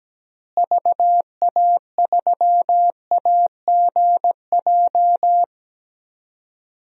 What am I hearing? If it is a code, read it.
Morse code, heard as a keyed tone, about 17 wpm, 700 Hz. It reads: VA3AGJ
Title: DXCC-100 Call Signs - 6 Characters - Repeated 3x - 17wpm